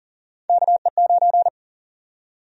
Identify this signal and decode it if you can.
Morse code, heard as a keyed tone, about 40 wpm, 700 Hz. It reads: KE9
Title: USA Call Sign Prefixes - 40wpm